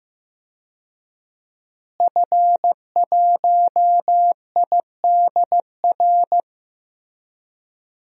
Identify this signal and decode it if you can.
Morse code, heard as a keyed tone, about 15 wpm, 700 Hz. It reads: F1IDR
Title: DXCC-100 Call Signs - 5 Characters - Repeated 3x - 15wpm